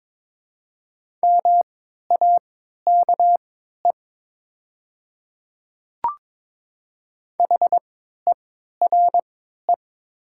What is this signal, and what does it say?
Morse code, heard as a keyed tone, about 22 wpm, spaced as for 12 wpm, 700 Hz. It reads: MAKE HERE
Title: 3x Character Spacing - Top 200 Words - 22wpm